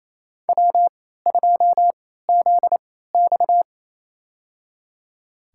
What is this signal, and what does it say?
Morse code, heard as a keyed tone, about 28 wpm, spaced as for 16 wpm, 700 Hz. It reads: W2ZX
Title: USA Call Signs - 4 Characters - 3x Character Spacing - 28wpm